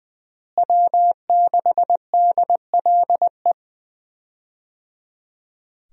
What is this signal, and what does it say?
Morse code, heard as a keyed tone, about 21 wpm, 700 Hz. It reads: W6DLE
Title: USA Call Signs - 5 Characters - Repeated 3x - 20wpm